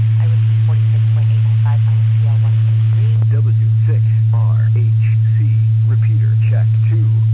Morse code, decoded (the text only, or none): none